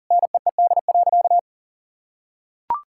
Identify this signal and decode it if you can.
Morse code, heard as a keyed tone, about 38 wpm, 700 Hz. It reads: NEED.
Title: Sentences from Top 1000 Words - Review Part 5 - 40wpm